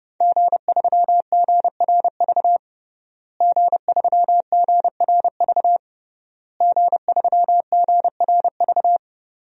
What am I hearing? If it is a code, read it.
Morse code, heard as a keyed tone, about 30 wpm, 700 Hz. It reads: G3GRV G3GRV G3GRV
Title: DXCC-100 Call Signs - 5 Characters - Repeated 3x - 30wpm